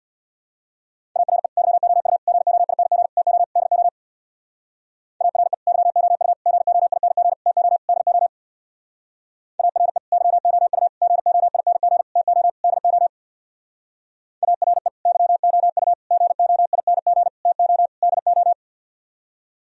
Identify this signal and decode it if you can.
Morse code, heard as a keyed tone, about 61 wpm, 700 Hz. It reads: ARE YOU GOING TO DO ARE YOU GOING TO DO ARE YOU GOING TO DO ARE YOU GOING TO DO
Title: Speed-Racing - Sets of 5 Words 50wpm